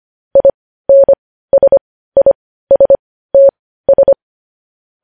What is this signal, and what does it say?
Morse code, heard as a keyed tone, about 25 wpm, spaced as for 15 wpm, 550 Hz. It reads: INSISTS
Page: news headlines in morse code at 15 wpm